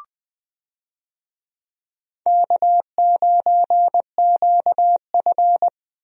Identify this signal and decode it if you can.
Morse code, heard as a keyed tone, about 20 wpm, 700 Hz. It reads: K9QF
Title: USA Call Signs - 4 Characters - 20wpm